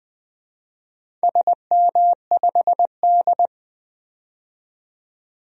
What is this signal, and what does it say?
Morse code, heard as a keyed tone, about 20 wpm, 700 Hz. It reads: SM5D
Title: DXCC-100 Call Signs - 4 Characters - 20wpm